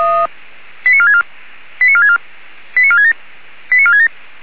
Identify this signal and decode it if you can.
Five-tone selective call (ZVEI), heard as quick runs of five tones, tones of about 70 ms. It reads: 89464 89464 89467 89467